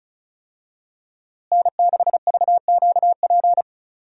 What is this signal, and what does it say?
Morse code, heard as a keyed tone, about 36 wpm, 700 Hz. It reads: N6VQP
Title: USA Call Signs - 5 Characters - Repeated 3x - 35wpm